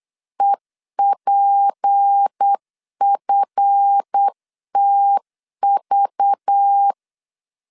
Morse code, heard as a keyed tone, about 8 wpm, 800 Hz. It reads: EPFTV
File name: LEZIONI / 7 GF9Z / 7 GF 8wpm 3108